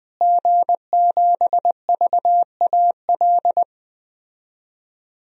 Morse code, heard as a keyed tone, about 20 wpm, 700 Hz. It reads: G7VAL
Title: DXCC-100 Call Signs - 5 Characters - Repeated 2x - 20wpm